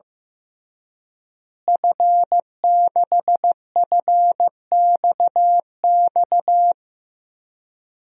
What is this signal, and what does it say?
Morse code, heard as a keyed tone, about 15 wpm, 700 Hz. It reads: F6FXX
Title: DXCC-100 Call Signs - 5 Characters - Repeated 3x - 15wpm